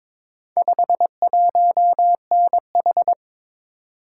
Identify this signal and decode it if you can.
Morse code, heard as a keyed tone, about 22 wpm, 700 Hz. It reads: 51NH
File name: practice / 22wpm / Four Letter-Number Combinations - 22wpm